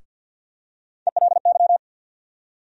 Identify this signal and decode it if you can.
Morse code, heard as a keyed tone, about 52 wpm, 700 Hz. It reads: ELY
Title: USA Call Sign Postfixes - 3 Letters 50wpm